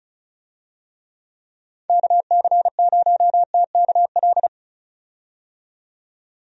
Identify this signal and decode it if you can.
Morse code, heard as a keyed tone, about 36 wpm, 700 Hz. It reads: KC0TKL
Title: USA Call Signs - 6 Characters - 35wpm